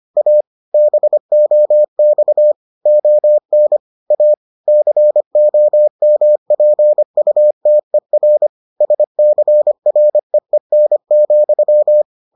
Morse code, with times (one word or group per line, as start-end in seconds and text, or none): 0.2-0.4 A
0.7-2.5 BOX
2.9-3.8 ON
4.1-4.3 A
4.7-8.5 COMPUTER
8.8-12.0 SCREEN,